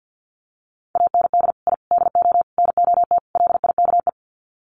Morse code, heard as a keyed tone, about 48 wpm, 700 Hz. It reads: AND I DO NOT LIKE
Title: Sets of 5 Words Encore2 50wpm